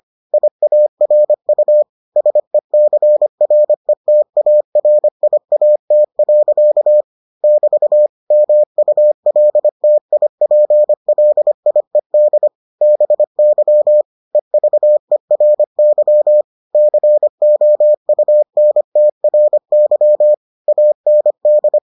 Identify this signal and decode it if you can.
Morse code, heard as a keyed tone, about 25 wpm, 600 Hz. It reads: IARU SECRETARIAT. <BT> MULTIPLIED BY EVERY COUNTRY AND